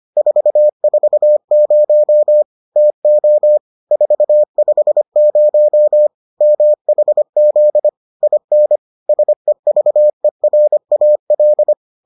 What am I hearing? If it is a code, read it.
Morse code, heard as a keyed tone, about 25 wpm, 600 Hz. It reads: 440 TO 450 MHZ IN SEVERAL